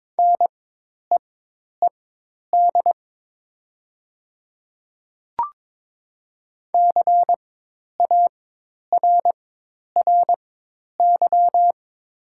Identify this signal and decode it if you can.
Morse code, heard as a keyed tone, about 22 wpm, spaced as for 10 wpm, 700 Hz. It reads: NEED CARRY
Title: 4x Character Spacing - Top 500 Words - 22wpm